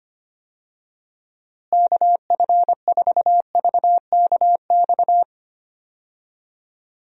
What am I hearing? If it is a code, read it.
Morse code, heard as a keyed tone, about 25 wpm, 700 Hz. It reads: KF4VKX
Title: USA Call Signs - 6 Characters - Repeated 2x - 25wpm